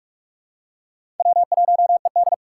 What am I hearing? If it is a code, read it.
Morse code, heard as a keyed tone, about 44 wpm, 700 Hz. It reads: W1ED
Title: USA Call Signs - 4 Characters - 6x Character Spacing - 45wpm